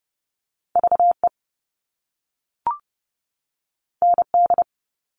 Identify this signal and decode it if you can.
Morse code, heard as a keyed tone, about 30 wpm, 700 Hz. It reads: VE ND
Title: All Cut Numbers - 2-Digit 30wpm